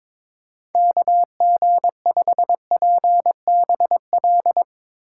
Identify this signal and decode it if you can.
Morse code, heard as a keyed tone, about 22 wpm, 700 Hz. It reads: KG5PBL
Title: Speed Racing - USA 2x3 Call Signs 22wpm